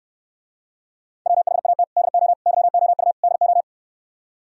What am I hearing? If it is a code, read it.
Morse code, heard as a keyed tone, about 69 wpm, 700 Hz. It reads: WHAT DO YOU DO